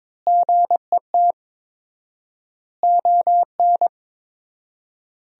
Morse code, heard as a keyed tone, about 22 wpm, 700 Hz. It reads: GET ON
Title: 4x Word-Spacing - Sentences from Top 200 Words - 22wpm